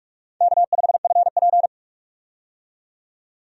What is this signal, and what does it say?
Morse code, heard as a keyed tone, about 42 wpm, 700 Hz. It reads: K5FP
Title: USA Call Signs - 4 Characters - 45wpm